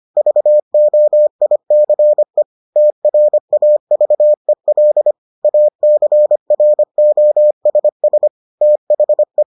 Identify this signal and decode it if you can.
Morse code, heard as a keyed tone, about 25 wpm, 600 Hz. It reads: VOICE TRAVEL ACROSS THE